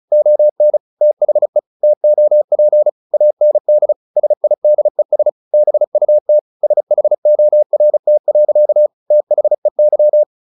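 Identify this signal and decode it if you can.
Morse code, heard as a keyed tone, about 34 wpm, 600 Hz. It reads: ON THE TOP AND SIDES BUT SHORT. THEY
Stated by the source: Morse practice at 35 wpm from CW med Gnister, sending Ken Follett's novel Never in English.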